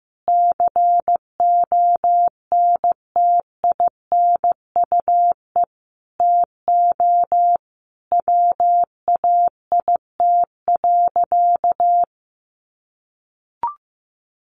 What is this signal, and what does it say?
Morse code, heard as a keyed tone, about 15 wpm, 700 Hz. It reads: CONTINUE TO WAIT.
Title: Easy Phrases 15wpm